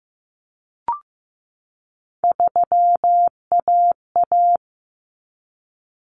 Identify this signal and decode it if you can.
Morse code, heard as a keyed tone, about 15 wpm, 700 Hz. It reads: 3AA
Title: Common Cut Numbers - 3-Digit 15wpm